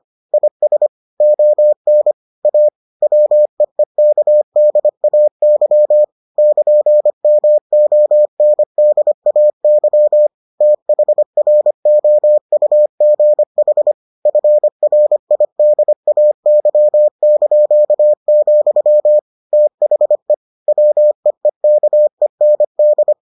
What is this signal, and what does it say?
Morse code, heard as a keyed tone, about 25 wpm, 600 Hz. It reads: IS ON A WEEKDAY <KN>MONDAY THROUGH FRIDAY), THE WEEKEND